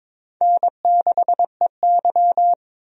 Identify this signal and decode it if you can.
Morse code, heard as a keyed tone, about 22 wpm, 700 Hz. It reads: N6EY
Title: USA Call Signs - 4 Characters - 22wpm